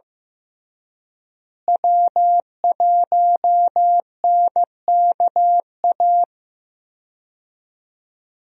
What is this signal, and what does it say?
Morse code, heard as a keyed tone, about 15 wpm, 700 Hz. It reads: W1NKA
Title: USA Call Signs - 5 Characters - Repeated 3x - 15wpm